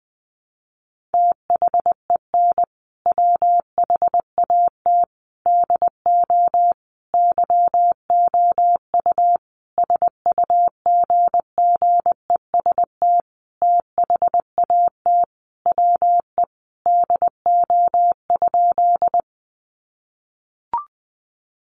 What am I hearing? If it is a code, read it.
Morse code, heard as a keyed tone, about 20 wpm, 700 Hz. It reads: THEN WHAT DO YOU SUGGEST THAT WE DO?
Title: Sentences from Top 1300 Words 20wpm